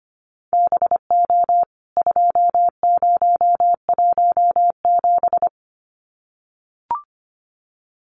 Morse code, heard as a keyed tone, about 25 wpm, 700 Hz. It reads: BO 2017
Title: CWT Exchange 25wpm